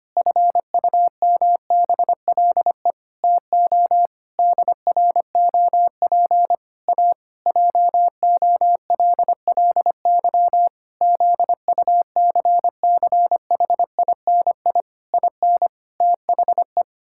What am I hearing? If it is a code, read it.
Morse code, heard as a keyed tone, about 25 wpm, 700 Hz. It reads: FUMBLE TO DROP A JOLLY ZUCCHINI IN THE
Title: Pangram Challenge v2 - 25wpm